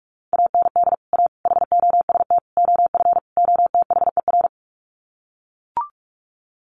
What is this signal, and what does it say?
Morse code, heard as a keyed tone, about 44 wpm, 700 Hz. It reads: AND A HOST OF OTHER
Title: Sets of 5 Words Encore2 45wpm